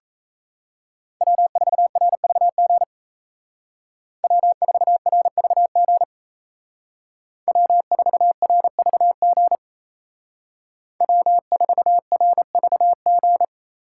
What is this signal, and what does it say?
Morse code, heard as a keyed tone, about 35 wpm, 700 Hz. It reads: W4RVG W4RVG W4RVG W4RVG
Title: Speed Racing - USA 1x3 Call Signs 28wpm